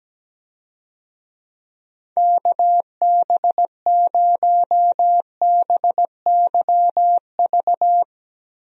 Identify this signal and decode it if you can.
Morse code, heard as a keyed tone, about 17 wpm, 700 Hz. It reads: KB0BYV
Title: USA Call Signs - 6 Characters - Repeated 3x - 17wpm